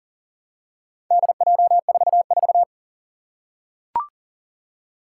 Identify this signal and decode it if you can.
Morse code, heard as a keyed tone, about 40 wpm, 700 Hz. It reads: DJ44
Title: Four Challenging Letter-Number Combinations 40wpm